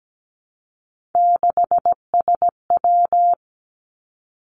Morse code, heard as a keyed tone, about 17 wpm, 700 Hz. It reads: 6SW
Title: Three Challenging Letter-Number Combinations 17wpm